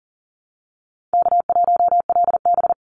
Morse code, heard as a keyed tone, about 42 wpm, 700 Hz. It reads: K1LB
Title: NA SOTA Activator Callsigns 40wpm